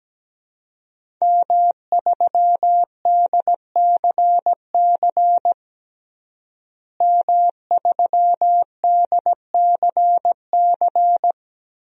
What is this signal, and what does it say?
Morse code, heard as a keyed tone, about 17 wpm, 700 Hz. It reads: M3DCC M3DCC
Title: DXCC-100 Call Signs - 5 Characters - Repeated 3x - 17wpm